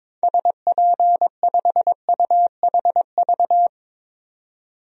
Morse code, heard as a keyed tone, about 22 wpm, 700 Hz. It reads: SP5UHV
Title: DXCC-100 Call Signs - 6 Characters - 22wpm